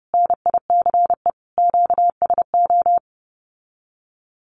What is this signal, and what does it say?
Morse code, heard as a keyed tone, about 30 wpm, 700 Hz. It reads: NICE QSO
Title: Common QSO Phrases Encore 30wpm